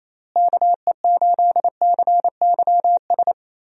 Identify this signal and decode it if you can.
Morse code, heard as a keyed tone, about 28 wpm, 700 Hz. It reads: KE8CYS
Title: USA Call Signs - 6 Characters - Repeated 2x - 28wpm